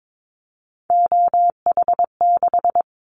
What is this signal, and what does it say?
Morse code, heard as a keyed tone, about 22 wpm, 700 Hz. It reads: OH6
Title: Callsign Prefixes for Top 50 DXCC - 22wpm